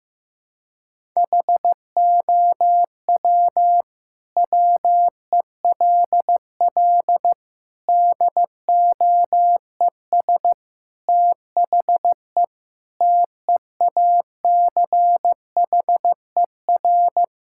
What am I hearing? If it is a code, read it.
Morse code, heard as a keyed tone, about 15 wpm, 700 Hz. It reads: HOW WELL DOES THE TEACHER
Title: Sentences from Top 1000 Words - Review Part 6 - 15wpm